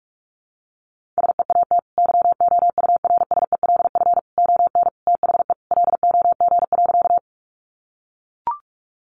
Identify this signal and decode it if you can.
Morse code, heard as a keyed tone, about 44 wpm, 700 Hz. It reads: SEAT YOURSELF ON THE LOG.
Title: Sentences from Top 1500 Words 45wpm